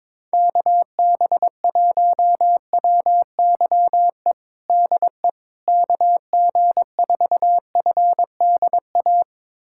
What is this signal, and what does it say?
Morse code, heard as a keyed tone, about 22 wpm, 700 Hz. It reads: KB1WYE DE KG4FDA
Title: USA Callsign DE Callsign - 6 Characters - 22wpm